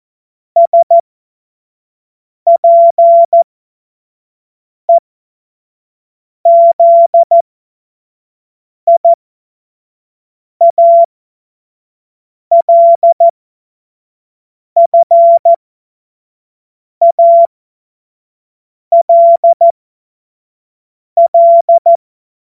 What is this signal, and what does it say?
Morse code, heard as a keyed tone, about 14 wpm, spaced as for 5 wpm, 680 Hz. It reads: SPEZIALFALL
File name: Morsekurs / Witze / Witz 07. Dezember 2021, 25 F70